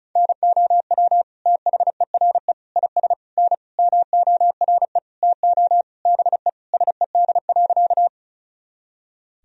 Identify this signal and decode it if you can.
Morse code, heard as a keyed tone, about 34 wpm, 700 Hz. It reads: NOW THERE IS N MORE TO BE SED.
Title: Sentences from Top 100 Words and Abbreviations 35wpm